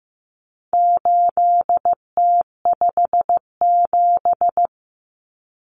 Morse code, heard as a keyed tone, about 15 wpm, 700 Hz. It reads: 8T57
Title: Common Cut Numbers - 4-Digit 15wpm